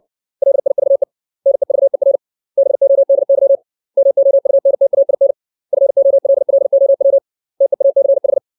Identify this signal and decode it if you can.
Morse code, heard as a keyed tone, about 60 wpm, 550 Hz. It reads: LIVE NEVER BODY MOUNTAIN FOLLOW TEACH
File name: 632 words / 632 Words 60WPM